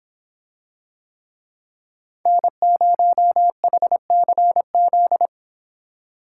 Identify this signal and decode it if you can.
Morse code, heard as a keyed tone, about 26 wpm, 700 Hz. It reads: N0HCZ